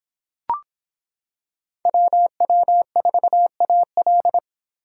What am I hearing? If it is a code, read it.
Morse code, heard as a keyed tone, about 25 wpm, 700 Hz. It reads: WW4AL